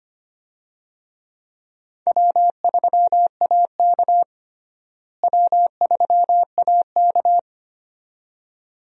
Morse code, heard as a keyed tone, about 25 wpm, 700 Hz. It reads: W3AK W3AK
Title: USA Call Signs - 4 Characters - Repeated 2x - 25wpm